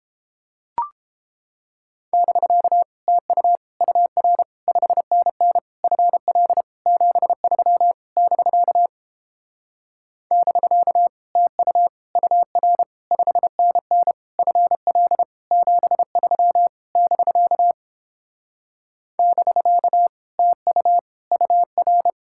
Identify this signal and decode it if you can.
Morse code, heard as a keyed tone, about 29 wpm, 700 Hz. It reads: <BK> TU UR 5NN FL 73 <BK> <BK> TU UR 5NN FL 73 <BK> <BK> TU UR